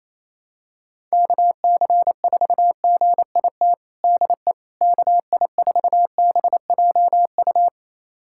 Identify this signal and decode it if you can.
Morse code, heard as a keyed tone, about 28 wpm, 700 Hz. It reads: KC4GIT DE KI4BJU
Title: USA Callsign DE Callsign 28wpm